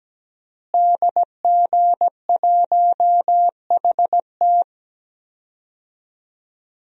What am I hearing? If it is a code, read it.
Morse code, heard as a keyed tone, about 17 wpm, 700 Hz. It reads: DG1HT
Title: DXCC-100 Call Signs - 5 Characters - Repeated 3x - 17wpm